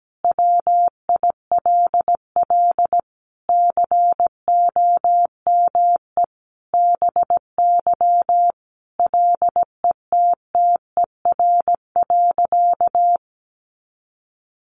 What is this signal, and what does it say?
Morse code, heard as a keyed tone, about 17 wpm, 700 Hz. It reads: WILL COME BY LETTER.
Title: Sentences from Top 1400 Words 17wpm